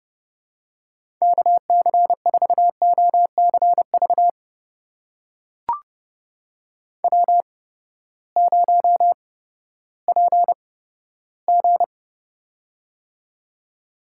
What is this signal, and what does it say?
Morse code, heard as a keyed tone, about 29 wpm, spaced as for 15 wpm, 700 Hz. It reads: KC4OCV W0PG
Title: USA Call Signs - 8x Character Spacing - 30wpm